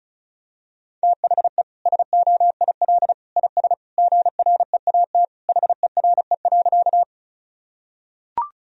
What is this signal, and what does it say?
Morse code, heard as a keyed tone, about 36 wpm, 700 Hz. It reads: THE SOIL IS GREAT HERE.
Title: Sentences from Top 1000 Words - Review Part 4 - 35wpm